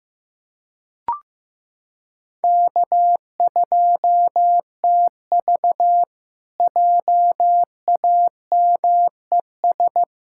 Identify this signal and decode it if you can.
Morse code, heard as a keyed tone, about 15 wpm, 700 Hz. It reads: K2TV JAMES
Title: K1USN SST QSO Practice - 15wpm